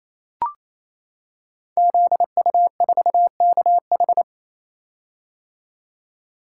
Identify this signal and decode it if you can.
Morse code, heard as a keyed tone, about 27 wpm, 700 Hz. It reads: ZU4KH
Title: Code Groups - 5 Letter:Numbers - 28wpm